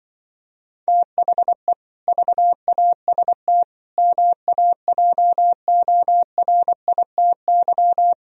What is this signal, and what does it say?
Morse code, heard as a keyed tone, about 24 wpm, 700 Hz. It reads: THE VAST MAJORITY
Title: Speed-Racing - Sets of 4 Words 20wpm